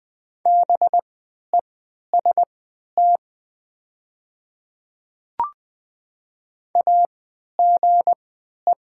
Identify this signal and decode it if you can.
Morse code, heard as a keyed tone, about 20 wpm, spaced as for 11 wpm, 700 Hz. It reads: BEST AGE
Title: 3x Character Spacing - Top 500 Words - 20wpm